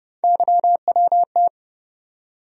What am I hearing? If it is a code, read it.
Morse code, heard as a keyed tone, about 30 wpm, 700 Hz. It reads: YWT
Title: USA Call Sign Postfixes - 3 Letters 30wpm